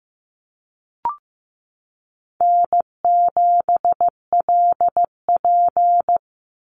Morse code, heard as a keyed tone, about 15 wpm, 700 Hz. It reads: N7LP
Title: NA SOTA Chaser Callsigns 15wpm